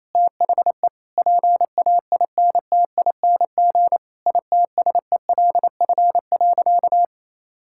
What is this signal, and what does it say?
Morse code, heard as a keyed tone, about 28 wpm, 700 Hz. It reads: THE PAINTING ITSELF.
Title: Sentences from Top 1000 Words - Review Part 6 - 28wpm